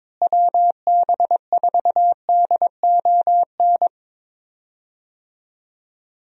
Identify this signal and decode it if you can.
Morse code, heard as a keyed tone, about 23 wpm, 700 Hz. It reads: WB4DON
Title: USA Call Signs - 6 Characters - Repeated 2x - 22wpm